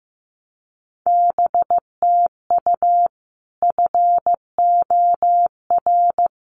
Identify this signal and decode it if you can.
Morse code, heard as a keyed tone, about 15 wpm, 700 Hz. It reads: BTU FOR
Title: Common QSO Phrases 15wpm